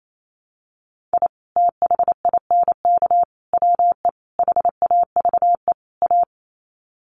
Morse code, heard as a keyed tone, about 28 wpm, 700 Hz. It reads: I THINK WE HAVE A
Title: Sets of 5 Words Encore 28wpm